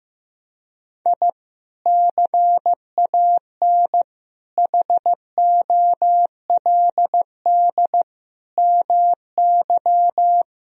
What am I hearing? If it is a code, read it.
Morse code, heard as a keyed tone, about 15 wpm, 700 Hz. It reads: I CAN HOLD MY
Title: Sentences from Top 1000 Words - Review Part 2 - 15wpm